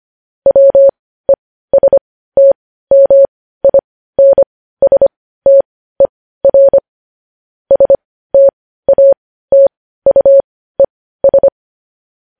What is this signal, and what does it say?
Morse code, heard as a keyed tone, about 25 wpm, spaced as for 15 wpm, 550 Hz. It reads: WESTMINSTER STATUES
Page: News Headlines in Morse Code at 15 WPM